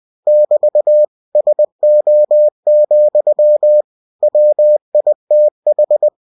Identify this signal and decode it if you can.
Morse code, heard as a keyed tone, about 20 wpm, 600 Hz. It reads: <BT> SO, WITH
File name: cw-books / winnie-the-pooh / winnie-20wpm-page-0020